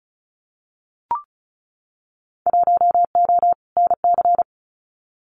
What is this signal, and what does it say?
Morse code, heard as a keyed tone, about 34 wpm, 700 Hz. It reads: 1O NC
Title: Winter Field Day Exchanges 35wpm